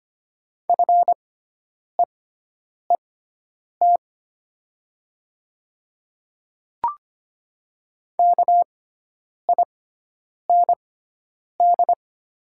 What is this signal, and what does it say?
Morse code, heard as a keyed tone, about 25 wpm, spaced as for 9 wpm, 690 Hz. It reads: FEET KIND
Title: 6x Character Spacing - Top 300 Words - 25wpm